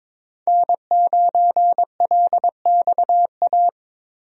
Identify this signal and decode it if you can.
Morse code, heard as a keyed tone, about 22 wpm, 700 Hz. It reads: N9LXA
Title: USA Call Signs - 5 Characters - 22wpm